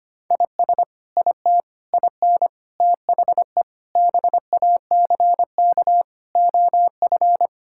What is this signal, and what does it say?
Morse code, heard as a keyed tone, about 25 wpm, 700 Hz. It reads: IS IT IN THE BACK OF
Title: Sentences from Top 1000 Words - Review Part 5 - 25wpm